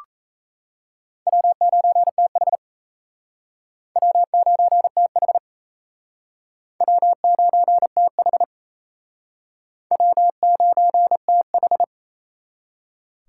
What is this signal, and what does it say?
Morse code, heard as a keyed tone, about 36 wpm, 700 Hz. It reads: W9TH W9TH W9TH W9TH